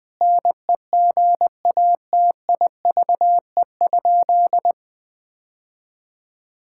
Morse code, heard as a keyed tone, about 21 wpm, 700 Hz. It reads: NEGATIVE?
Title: Sentences from Top 1000 Words - Review Part 6 - 20wpm